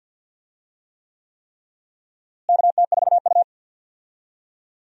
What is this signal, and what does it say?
Morse code, heard as a keyed tone, about 49 wpm, 700 Hz. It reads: KT4U
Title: USA Call Signs - 4 Characters - Repeated 3x - 50wpm